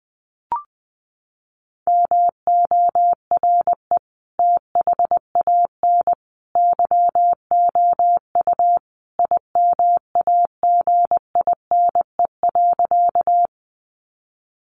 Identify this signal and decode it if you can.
Morse code, heard as a keyed tone, about 20 wpm, 700 Hz. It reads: MORE THAN YOU IMAGINE.
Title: Sentences from Top 1400 Words 20wpm